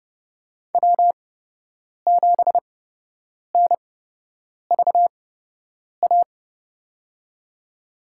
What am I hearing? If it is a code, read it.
Morse code, heard as a keyed tone, about 29 wpm, spaced as for 8 wpm, 700 Hz. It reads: W7NVA
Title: USA Call Signs - 5 Characters - 8x Character Spacing - 30wpm